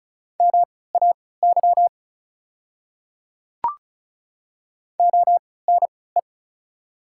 Morse code, heard as a keyed tone, about 35 wpm, spaced as for 20 wpm, 690 Hz. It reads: MAY ONE
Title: 3x Character Spacing - Top 100 Words - 35wpm